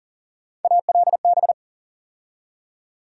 Morse code, heard as a keyed tone, about 40 wpm, 700 Hz. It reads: ALB
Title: USA Call Sign Postfixes - 3 Letters 40wpm